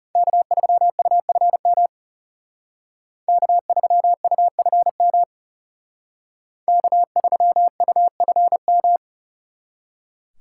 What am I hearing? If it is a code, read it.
Morse code, heard as a keyed tone, about 36 wpm, 700 Hz. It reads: K3UFM K3UFM K3UFM
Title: Speed Racing - USA Call Signs 30wpm